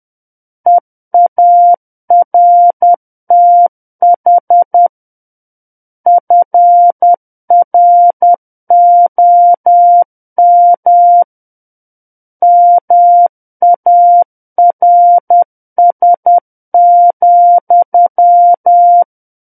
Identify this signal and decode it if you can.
Morse code, heard as a keyed tone, about 10 wpm, 700 Hz. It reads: EARTH FROM MARS,
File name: Book One / 01-Chapter 1 - 10 WPM